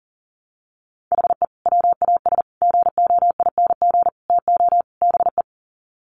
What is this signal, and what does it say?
Morse code, heard as a keyed tone, about 40 wpm, 700 Hz. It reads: HE WAS GOING TO BE
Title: Sets of 5 Words 40wpm